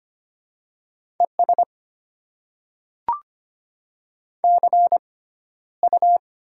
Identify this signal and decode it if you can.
Morse code, heard as a keyed tone, about 25 wpm, spaced as for 13 wpm, 700 Hz. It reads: ES CU
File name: Top 20 QSO Elements - 6x Character Spacing - 25wpm